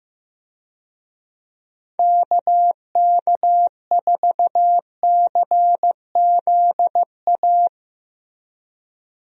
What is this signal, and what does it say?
Morse code, heard as a keyed tone, about 15 wpm, 700 Hz. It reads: KK4CZA
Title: USA Call Signs - 6 Characters - Repeated 3x - 15wpm